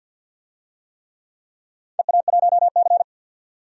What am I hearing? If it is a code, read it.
Morse code, heard as a keyed tone, about 46 wpm, 700 Hz. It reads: EA1C
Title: DXCC-100 Call Signs - 4 Characters - 50wpm